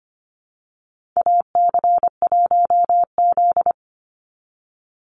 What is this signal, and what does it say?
Morse code, heard as a keyed tone, about 25 wpm, 700 Hz. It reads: AC1Z
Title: NA SOTA Chaser Callsigns 25wpm